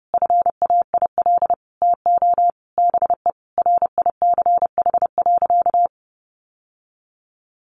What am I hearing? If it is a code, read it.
Morse code, heard as a keyed tone, about 28 wpm, 700 Hz. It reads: FAIL TO BE RICH.
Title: Sentences from Top 1100 Words 30wpm